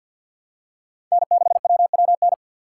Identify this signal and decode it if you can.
Morse code, heard as a keyed tone, about 52 wpm, 700 Hz. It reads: N6WWN
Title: USA Call Signs - 5 Characters - 4x Character Spacing - 50wpm